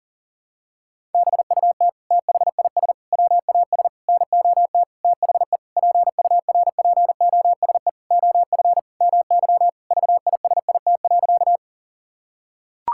Morse code, heard as a keyed tone, about 40 wpm, 700 Hz. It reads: BUT THIS WAS NOT THE PURPOSE OF MY VISIT.
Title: Sentences from Top 1000 Words - Review Part 6 - 40wpm